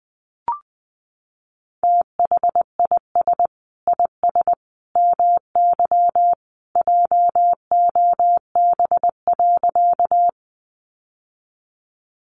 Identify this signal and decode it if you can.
Morse code, heard as a keyed tone, about 20 wpm, 700 Hz. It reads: THIS IS MY JOB.
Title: Sentences from Top 600 Words 20wpm